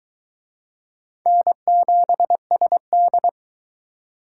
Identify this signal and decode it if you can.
Morse code, heard as a keyed tone, about 22 wpm, 700 Hz. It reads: N7SD